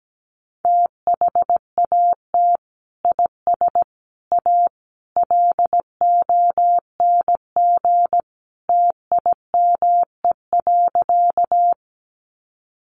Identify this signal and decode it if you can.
Morse code, heard as a keyed tone, about 17 wpm, 700 Hz. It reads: THAT IS A LONG TIME.
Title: Sentences from Top 100 Words 17wpm